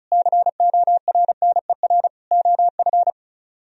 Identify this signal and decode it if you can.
Morse code, heard as a keyed tone, about 44 wpm, 700 Hz. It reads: CORNER OF
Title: Speed-Racing - Sets of 5 Words 35wpm